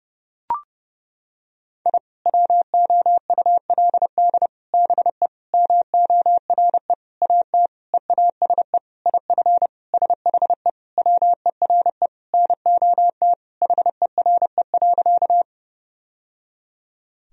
Morse code, heard as a keyed tone, about 30 wpm, 700 Hz. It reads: I WOULD BE MORE AT EASE IF SHE WERE NOT HERE.
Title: Sentences from Top 1000 Words - Review Part 1 - 30wpm